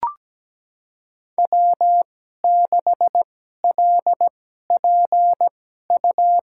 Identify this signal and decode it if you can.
Morse code, heard as a keyed tone, about 17 wpm, spaced as for 12 wpm, 700 Hz. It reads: W6LPU
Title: USA Call Signs - 2x Character Spacing - 17wpm